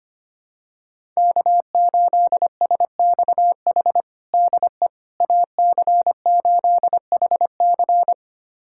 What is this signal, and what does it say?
Morse code, heard as a keyed tone, about 25 wpm, 700 Hz. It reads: K8SXH DE AC8HC
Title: USA Callsign DE Callsign 25wpm